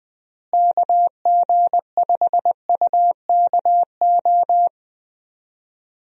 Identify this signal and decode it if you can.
Morse code, heard as a keyed tone, about 20 wpm, 700 Hz. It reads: KG5UKO
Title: USA Call Signs - 6 Characters - Repeated 3x - 20wpm